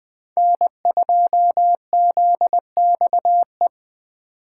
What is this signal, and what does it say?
Morse code, heard as a keyed tone, about 20 wpm, 700 Hz. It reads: N2ZXE
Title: Speed Racing - USA 1x3 Call Signs 17wpm